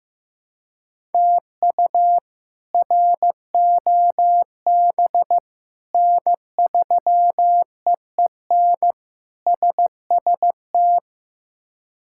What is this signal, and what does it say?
Morse code, heard as a keyed tone, about 15 wpm, 700 Hz. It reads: TU ROB N3EEN SST
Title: K1USN SST QSO Practice - 15wpm